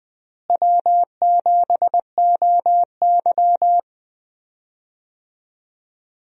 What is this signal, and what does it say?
Morse code, heard as a keyed tone, about 21 wpm, 700 Hz. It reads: W7OY
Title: USA Call Signs - 4 Characters - Repeated 3x - 20wpm